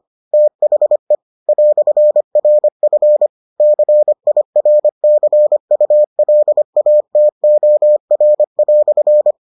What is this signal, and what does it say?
Morse code, heard as a keyed tone, about 25 wpm, 600 Hz. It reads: THE "RF CIRCULATOR"